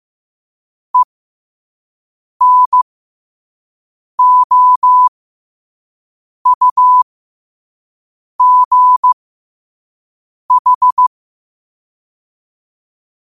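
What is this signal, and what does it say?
Morse code, heard as a keyed tone, about 15 wpm, spaced as for 5 wpm, 1000 Hz. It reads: ENOUGH
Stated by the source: Morse code sent at 5 words per minute.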